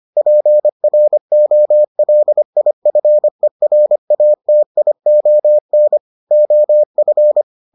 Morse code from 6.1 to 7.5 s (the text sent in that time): OF